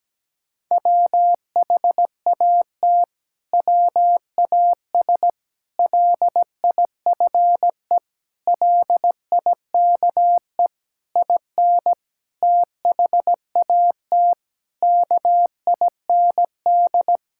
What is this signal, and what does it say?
Morse code, heard as a keyed tone, about 17 wpm, 700 Hz. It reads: WHAT WAS LIFE LIKE IN THAT KIND
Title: Sentences from Top 1000 Words - Review Part 5 - 17wpm